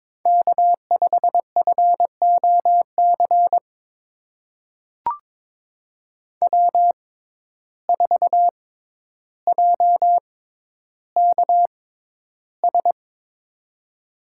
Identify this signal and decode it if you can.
Morse code, heard as a keyed tone, about 22 wpm, spaced as for 11 wpm, 700 Hz. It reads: K5FOC W4JKS
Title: USA Call Signs - 5 Characters - 6x Character Spacing - 22wpm